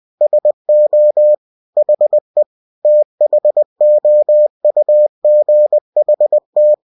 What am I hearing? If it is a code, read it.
Morse code, heard as a keyed tone, about 20 wpm, 600 Hz. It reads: SO HE THOUGHT